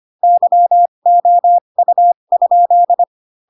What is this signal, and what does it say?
Morse code, heard as a keyed tone, about 25 wpm, spaced as for 22 wpm, 700 Hz. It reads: YOU?